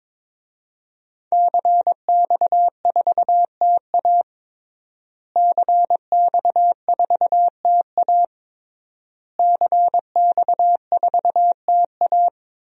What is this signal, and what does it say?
Morse code, heard as a keyed tone, about 21 wpm, 700 Hz. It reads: CX4TA CX4TA CX4TA
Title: DXCC-100 Call Signs - 5 Characters - Repeated 3x - 22wpm